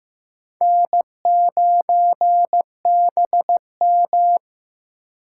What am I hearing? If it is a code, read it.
Morse code, heard as a keyed tone, about 15 wpm, 700 Hz. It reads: N9BM
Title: USA Call Signs - 4x Character Spacing - 15wpm